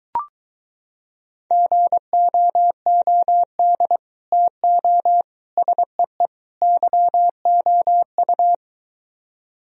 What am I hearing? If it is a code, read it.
Morse code, heard as a keyed tone, about 24 wpm, 700 Hz. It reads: GOOD TO SEE YOU